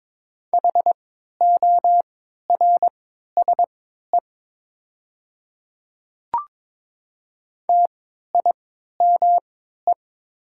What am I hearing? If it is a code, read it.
Morse code, heard as a keyed tone, about 22 wpm, spaced as for 13 wpm, 690 Hz. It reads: HORSE TIME
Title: 3x Character Spacing - Top 300 Words - 22wpm